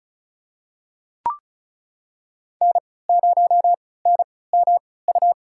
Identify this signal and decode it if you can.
Morse code, heard as a keyed tone, about 34 wpm, spaced as for 20 wpm, 700 Hz. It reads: N0NMU
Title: USA Call Signs - 3x Character Spacing - 35wpm